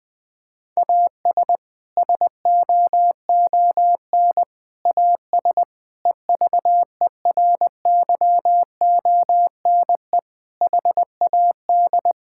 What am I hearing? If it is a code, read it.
Morse code, heard as a keyed tone, about 19 wpm, 700 Hz. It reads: AS SOON AS EVERYONE HAD
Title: Sentences from Top 1000 Words - Review Part 5 - 20wpm